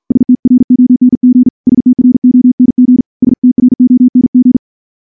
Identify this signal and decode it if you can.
Morse code, heard as a keyed tone, about 46 wpm, 250 Hz. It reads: STRONG STRONG STRONG
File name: morse / morse-training-50WPM